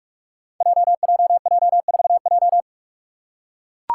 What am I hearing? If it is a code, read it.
Morse code, heard as a keyed tone, about 44 wpm, 700 Hz. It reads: JJJ4J
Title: Five Challenging Letter-Number Combinations 45wpm